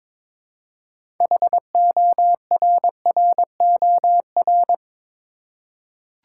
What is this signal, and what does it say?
Morse code, heard as a keyed tone, about 22 wpm, 700 Hz. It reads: HORROR
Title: Six Letter Words - 22wpm